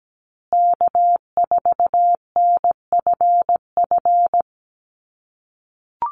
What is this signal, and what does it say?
Morse code, heard as a keyed tone, about 17 wpm, 700 Hz. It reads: K4NFF
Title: USA Call Signs Repeated 2x 17wpm